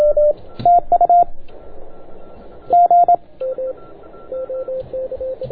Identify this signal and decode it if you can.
Morse code, heard as a keyed tone, about 27 wpm, 690 Hz. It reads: TU G